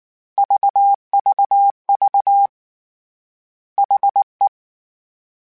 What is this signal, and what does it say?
Morse code, heard as a keyed tone, about 19 wpm, 800 Hz. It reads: VVV HE
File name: H Prefixes / H mp3s